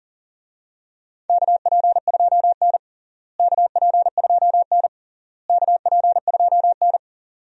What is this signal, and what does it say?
Morse code, heard as a keyed tone, about 40 wpm, 700 Hz. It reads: KP2N KP2N KP2N
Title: DXCC-100 Call Signs - 4 Characters - Repeated 3x - 40wpm